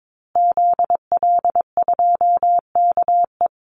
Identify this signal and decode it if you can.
Morse code, heard as a keyed tone, about 22 wpm, 700 Hz. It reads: ZL2KE
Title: Call Signs Repeated 2x - 22wpm